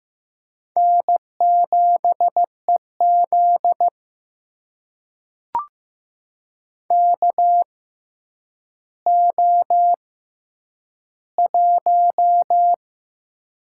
Morse code, heard as a keyed tone, about 15 wpm, spaced as for 9 wpm, 700 Hz. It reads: N7EZ KO1
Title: USA Call Signs - 4 Characters - 6x Character Spacing - 15wpm